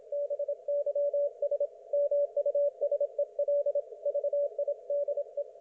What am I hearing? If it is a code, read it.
Morse code, heard as a keyed tone, about 26 wpm, 560 Hz. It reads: BYS MUSEL VIDE
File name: Archiv / CUC179 / ok1fhi cul179